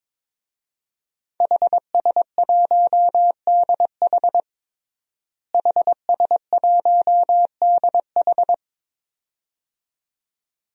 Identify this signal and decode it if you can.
Morse code, heard as a keyed tone, about 22 wpm, 700 Hz. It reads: HS1DH HS1DH
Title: DXCC-100 Call Signs - 5 Characters - Repeated 2x - 22wpm